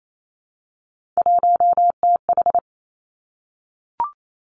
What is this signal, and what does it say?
Morse code, heard as a keyed tone, about 28 wpm, 700 Hz. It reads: 1TH
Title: Three Challenging Letter-Number Combinations 28wpm